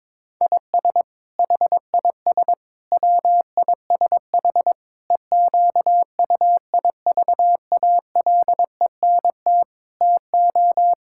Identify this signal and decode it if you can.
Morse code, heard as a keyed tone, about 22 wpm, 700 Hz. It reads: IS HIS WISH EQUIVALENT TO
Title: Sentences from Top 1600 Words 22wpm